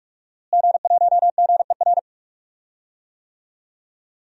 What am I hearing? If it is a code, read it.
Morse code, heard as a keyed tone, about 41 wpm, 700 Hz. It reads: G1GER
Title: DXCC-100 Call Signs - 5 Characters - Repeated 3x - 45wpm